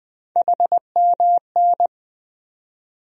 Morse code, heard as a keyed tone, about 20 wpm, 700 Hz. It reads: HMN
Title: USA Call Sign Postfixes - 3 Letters 20wpm